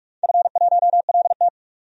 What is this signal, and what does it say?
Morse code, heard as a keyed tone, about 44 wpm, 700 Hz. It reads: F1LT
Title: DXCC-100 Call Signs - 4 Characters - 45wpm